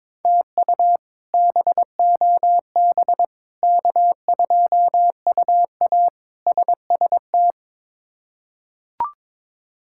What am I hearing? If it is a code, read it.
Morse code, heard as a keyed tone, about 22 wpm, 700 Hz. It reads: TU BOB K2UA SST
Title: K1USN SST QSO Practice - 22wpm